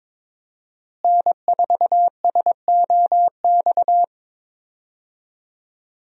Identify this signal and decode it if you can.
Morse code, heard as a keyed tone, about 22 wpm, 700 Hz. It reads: N4SOX
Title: USA Call Signs - 5 Characters - Repeated 3x - 22wpm